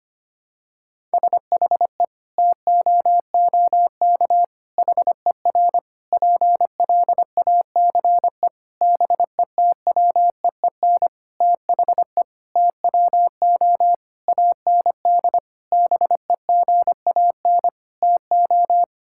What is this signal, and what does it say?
Morse code, heard as a keyed tone, about 25 wpm, 700 Hz. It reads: SHE TOOK HER PLACE BETWEEN THE TWO AND BEGAN TO
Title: Sentences from Top 1000 Words - Review Part 5 - 25wpm